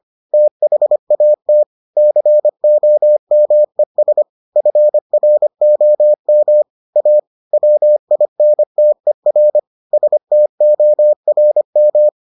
Morse code, heard as a keyed tone, about 25 wpm, 600 Hz. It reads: THAT COMES FROM A WINTER STORM